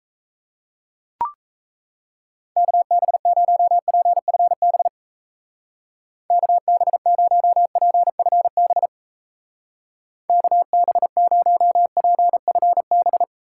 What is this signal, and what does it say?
Morse code, heard as a keyed tone, about 37 wpm, 700 Hz. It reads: KB0PFB KB0PFB KB0PFB